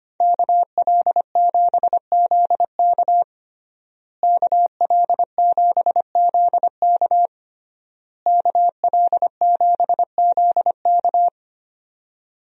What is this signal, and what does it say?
Morse code, heard as a keyed tone, about 25 wpm, 700 Hz. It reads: KL7ZK KL7ZK KL7ZK
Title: USA Call Signs - 5 Characters - Repeated 3x - 25wpm